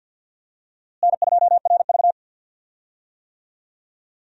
Morse code, heard as a keyed tone, about 52 wpm, 700 Hz. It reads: N2RV
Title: USA Call Signs - 4 Characters - Repeated 3x - 50wpm